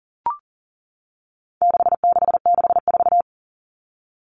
Morse code, heard as a keyed tone, about 40 wpm, 700 Hz. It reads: 6664
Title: Numbers - 4-Digit 40wpm